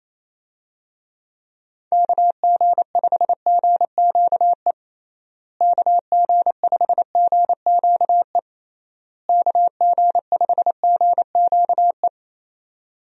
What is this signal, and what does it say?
Morse code, heard as a keyed tone, about 28 wpm, 700 Hz. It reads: KG5GQE KG5GQE KG5GQE
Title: USA Call Signs - 6 Characters - Repeated 3x - 28wpm